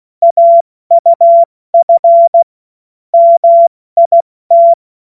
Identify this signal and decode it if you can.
Morse code, heard as a keyed tone, about 16 wpm, 680 Hz. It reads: AUF MIT
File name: Morsekurs / Witze / Witz 03. Mai 2022 70 F80